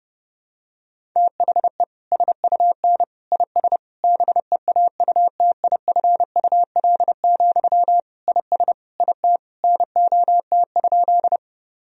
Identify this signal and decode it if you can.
Morse code, heard as a keyed tone, about 30 wpm, 700 Hz. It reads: THE SUN IS BEAUTIFUL, IS IT NOT?
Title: Sentences from Top 1000 Words - Review Part 1 - 30wpm